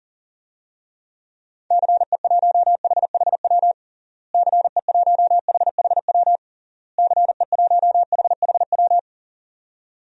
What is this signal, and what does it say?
Morse code, heard as a keyed tone, about 37 wpm, 700 Hz. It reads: CE1HHW CE1HHW CE1HHW
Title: DXCC-100 Call Signs - 6 Characters - Repeated 3x - 40wpm